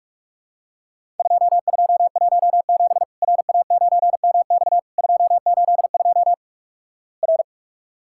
Morse code, heard as a keyed tone, about 46 wpm, 700 Hz. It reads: 2217 RA9MX 282 R
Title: Worked All Europe DX Contest - QTC Exchange - 45wpm